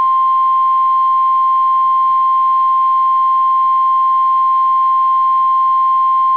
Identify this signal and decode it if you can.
Emergency Alert System (SAME) tones (the 1050 Hz attention tone). It is heard from 0.8 to 6.1 s.